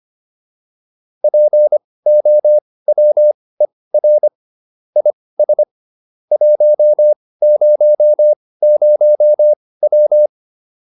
Morse code, heard as a keyed tone, about 25 wpm, spaced as for 18 wpm, 600 Hz. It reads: POWER IS 100W